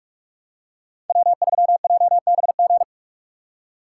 Morse code, heard as a keyed tone, about 44 wpm, 700 Hz. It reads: W3JBG